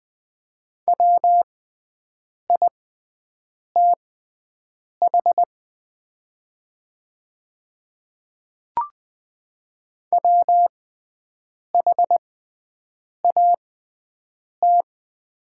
Morse code, heard as a keyed tone, about 20 wpm, spaced as for 7 wpm, 700 Hz. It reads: WITH WHAT
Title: 6x Character Spacing - Top 200 Words - 20wpm